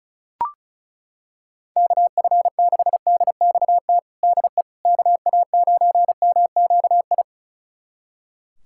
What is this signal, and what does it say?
Morse code, heard as a keyed tone, about 34 wpm, 700 Hz. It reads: KF6DXT DE KA9MQI
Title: USA Callsign DE Callsign - 6 Characters - 35wpm